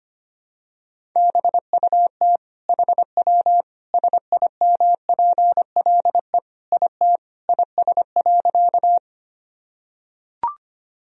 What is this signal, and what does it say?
Morse code, heard as a keyed tone, about 25 wpm, 700 Hz. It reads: BUT HW SIMPLE IT IS.
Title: Sentences from Top 400 Words and Abbreviations 25wpm